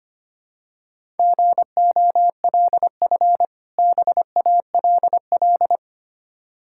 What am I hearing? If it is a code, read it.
Morse code, heard as a keyed tone, about 25 wpm, 690 Hz. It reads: GOLF BALL
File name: practice / 25wpm / Difficult Words with Hints - 25wpm